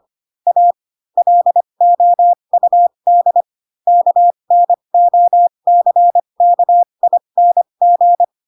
Morse code, heard as a keyed tone, about 25 wpm, spaced as for 22 wpm, 700 Hz. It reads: A LOUD KNOCKING
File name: Pinocchio / Pinocchio ch6 18 25 ik1hge